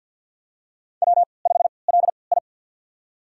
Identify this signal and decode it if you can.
Morse code, heard as a keyed tone, about 52 wpm, spaced as for 29 wpm, 700 Hz. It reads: W5LI
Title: USA Call Signs - 4 Characters - 3x Character Spacing - 50wpm